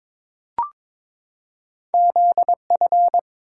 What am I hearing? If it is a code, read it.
Morse code, heard as a keyed tone, about 23 wpm, 700 Hz. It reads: ZF
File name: practice / 22wpm / Two Letter-Number Combinations - 22wpm